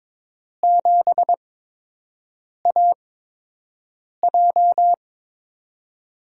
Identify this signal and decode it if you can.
Morse code, heard as a keyed tone, about 22 wpm, spaced as for 6 wpm, 700 Hz. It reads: 7AJ
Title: USA Call Signs - 4 Characters - 8x Character Spacing - 22wpm